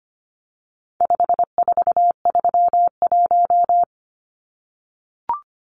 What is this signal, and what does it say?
Morse code, heard as a keyed tone, about 25 wpm, 700 Hz. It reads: 5431
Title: Numbers - 4-Digit 25wpm